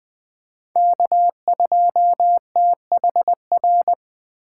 Morse code, heard as a keyed tone, about 20 wpm, 700 Hz. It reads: K2THR
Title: USA Call Signs - 8x Character Spacing - 20wpm